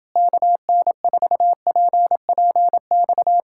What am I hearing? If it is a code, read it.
Morse code, heard as a keyed tone, about 29 wpm, 700 Hz. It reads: KN4PPX